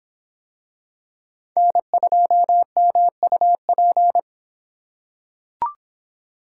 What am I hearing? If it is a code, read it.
Morse code, heard as a keyed tone, about 26 wpm, 700 Hz. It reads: N2MUP